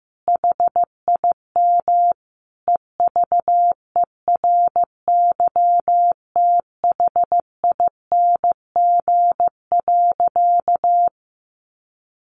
Morse code, heard as a keyed tone, about 15 wpm, 700 Hz. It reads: HIM EVERYTHING.
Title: Sentences from Top 700 Words 15wpm